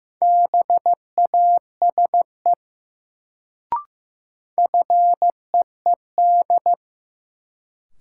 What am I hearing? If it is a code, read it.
Morse code, heard as a keyed tone, about 15 wpm, 700 Hz. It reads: BASE FEED
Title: Sentences from Top 1600 Words - New Word Drill Alt - 15wpm